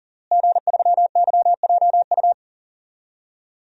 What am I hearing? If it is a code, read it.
Morse code, heard as a keyed tone, about 37 wpm, 700 Hz. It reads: G3YJU
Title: DXCC-100 Call Signs - 5 Characters - Repeated 3x - 40wpm